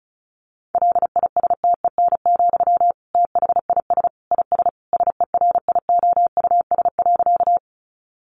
Morse code, heard as a keyed tone, about 34 wpm, 700 Hz. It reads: LISTEN, THIS IS SERIOUS.
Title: Sentences from Top 1100 Words 35wpm